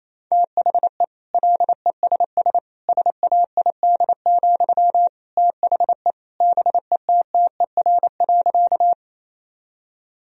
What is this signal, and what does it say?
Morse code, heard as a keyed tone, about 28 wpm, 700 Hz. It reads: THE LESS SAID, THE BETTER.
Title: Sentences from Top 1000 Words - Review Part 2 - 28wpm